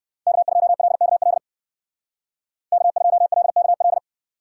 Morse code, heard as a keyed tone, about 64 wpm, 700 Hz. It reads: K2LKL K2LKL